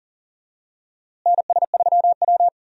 Morse code, heard as a keyed tone, about 40 wpm, 700 Hz. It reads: NS3W
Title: USA Call Signs - 4 Characters - Repeated 3x - 40wpm